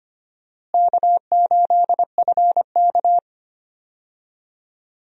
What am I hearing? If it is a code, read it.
Morse code, heard as a keyed tone, about 25 wpm, 700 Hz. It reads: K8FK
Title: USA Call Signs - 4 Characters - Repeated 2x - 25wpm